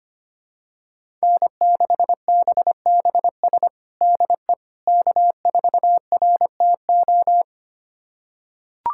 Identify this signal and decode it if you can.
Morse code, heard as a keyed tone, about 25 wpm, 700 Hz. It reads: N6BBS DE K4RTO
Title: USA Callsign DE Callsign - 5 Characters - 25wpm